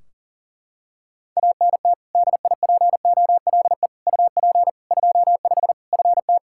Morse code, heard as a keyed tone, about 40 wpm, 700 Hz. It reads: ANT DIPOLE UP 25 FT
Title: Practice QSO - Exchange 2 - Info - 40wpm